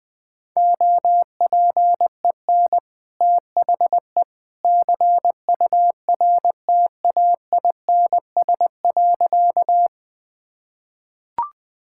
Morse code, heard as a keyed tone, about 21 wpm, 700 Hz. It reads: OPEN THE CURTAINS.
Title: Easy Sentences with Hints - 20wpm